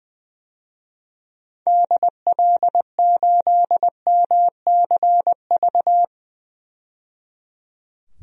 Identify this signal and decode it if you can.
Morse code, heard as a keyed tone, about 20 wpm, 700 Hz. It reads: DL8MCV
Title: DXCC-100 Call Signs - 6 Characters - 20wpm